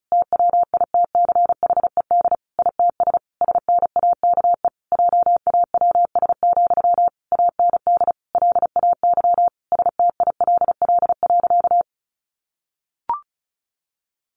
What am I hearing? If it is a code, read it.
Morse code, heard as a keyed tone, about 34 wpm, 700 Hz. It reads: TWITCHED ITS SNAKE JAWS, AND LAY STILL.
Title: A Sound Of Thunder 35wpm